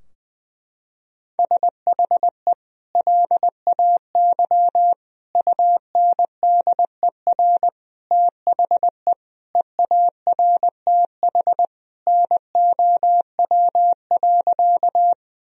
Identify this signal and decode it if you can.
Morse code, heard as a keyed tone, about 21 wpm, 700 Hz. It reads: SHE LAY UNDER THE EARTH NOW.
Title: Sentences from Top 1000 Words - Review Part 2 - 20wpm